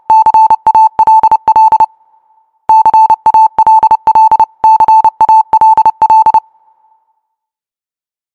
Morse code, heard as a keyed tone, about 29 wpm, 880 Hz. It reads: CALL CALL CALL